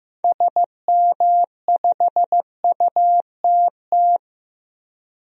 Morse code, heard as a keyed tone, about 15 wpm, 700 Hz. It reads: SM5UTT
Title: DXCC-100 Call Signs - 6 Characters - 15wpm